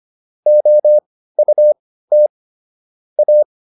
Morse code, heard as a keyed tone, about 25 wpm, spaced as for 15 wpm, 600 Hz. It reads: OUT A